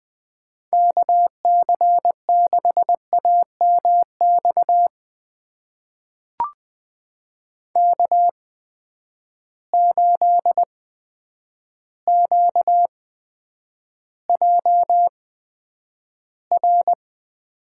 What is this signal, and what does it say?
Morse code, heard as a keyed tone, about 20 wpm, spaced as for 9 wpm, 700 Hz. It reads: KC6AMX K8QJR
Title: USA Call Signs - 8x Character Spacing - 20wpm